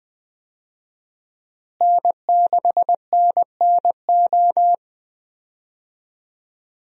Morse code, heard as a keyed tone, about 20 wpm, 700 Hz. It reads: N6NNO
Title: USA Call Signs - 5 Characters - Repeated 3x - 20wpm